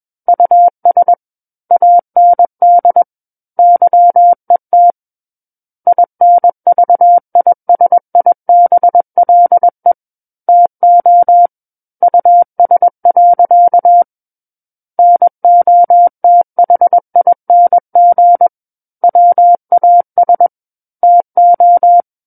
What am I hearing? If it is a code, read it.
Morse code, heard as a keyed tone, about 21 wpm, 700 Hz. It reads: US AND YET INVISIBLE TO US. NOTHING WAS TO
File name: Book One / 12-Chapter 12 - 21 WPM